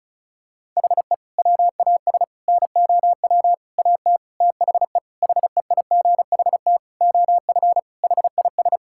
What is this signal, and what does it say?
Morse code, heard as a keyed tone, about 34 wpm, 700 Hz. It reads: HE WAS NOW AT THE HEIGHT OF HIS
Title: Sentences from Top 1600 Words 35wpm